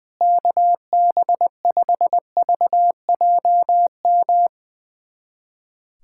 Morse code, heard as a keyed tone, about 21 wpm, 700 Hz. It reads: KB5VJM
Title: USA Callsign DE Callsign 20wpm